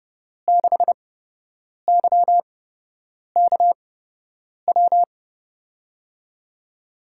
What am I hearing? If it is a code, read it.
Morse code, heard as a keyed tone, about 30 wpm, spaced as for 8 wpm, 700 Hz. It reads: 6YKW
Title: USA Call Signs - 6 Characters - 8x Character Spacing - 30wpm